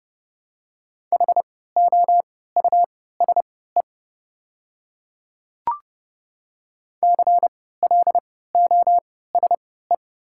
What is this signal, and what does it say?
Morse code, heard as a keyed tone, about 30 wpm, spaced as for 17 wpm, 700 Hz. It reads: HOUSE CLOSE
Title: 3x Character Spacing - Top 300 Words - 30wpm